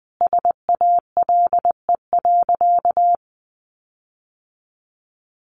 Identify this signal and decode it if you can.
Morse code, heard as a keyed tone, about 20 wpm, 700 Hz. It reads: SALE.
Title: Sentences from Top 1400 Words 20wpm